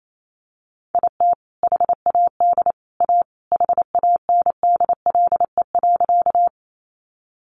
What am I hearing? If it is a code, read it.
Morse code, heard as a keyed tone, about 28 wpm, 700 Hz. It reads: IT HAD A HANDLE.
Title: Sentences from Top 1100 Words 28wpm